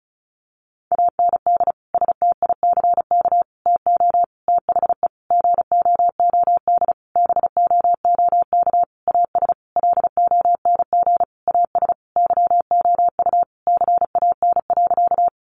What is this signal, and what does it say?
Morse code, heard as a keyed tone, about 34 wpm, 700 Hz. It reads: AND STICK TO THE GOOD BOOK AS LONG AS YOU CAN.
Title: Sentences from Top 1200 Words 35wpm